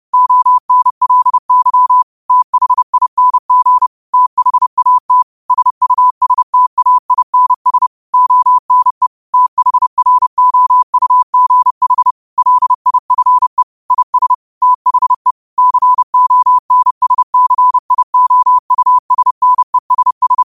Morse code, heard as a keyed tone, about 30 wpm, 1000 Hz. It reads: ONLY THING THAT SUSTAINS ONE THROUGH LIFE IS THE CONSCIOUSNESS
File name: qotdfiles / 30wpm